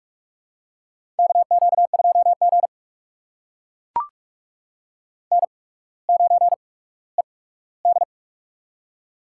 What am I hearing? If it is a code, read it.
Morse code, heard as a keyed tone, about 42 wpm, spaced as for 19 wpm, 700 Hz. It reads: KQ2G N9ED
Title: USA Call Signs - 4 Characters - 8x Character Spacing - 45wpm